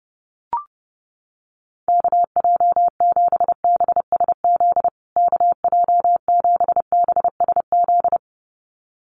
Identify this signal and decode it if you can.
Morse code, heard as a keyed tone, about 30 wpm, 700 Hz. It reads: KJ7BSZ KJ7BSZ
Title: USA Call Signs Repeated 2x 30wpm